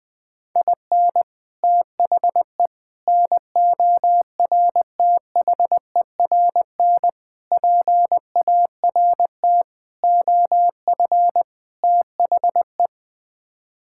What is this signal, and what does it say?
Morse code, heard as a keyed tone, about 20 wpm, 700 Hz. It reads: IN THE NORTHERN PART OF THE
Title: Sets of 6 Words - 20wpm